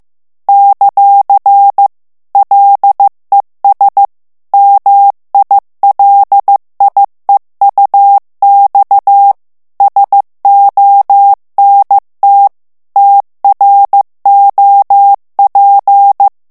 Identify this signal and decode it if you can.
Morse code, heard as a keyed tone, about 15 wpm, 800 Hz. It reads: ; LES MILIEUX SONT TROP
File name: FRANCAIS TEXTE CLAIR 2 VITESSE 14 MOTS MINUTES